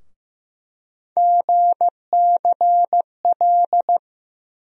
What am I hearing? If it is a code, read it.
Morse code, heard as a keyed tone, about 15 wpm, 700 Hz. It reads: GCL
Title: USA Call Sign Postfixes - 3 Letters 15wpm